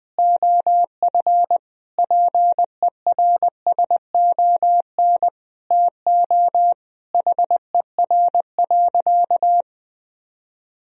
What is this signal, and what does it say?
Morse code, heard as a keyed tone, about 20 wpm, 700 Hz. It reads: OF PERSON TO HER.
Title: Sentences from Top 1000 Words - Review Part 3 - 20wpm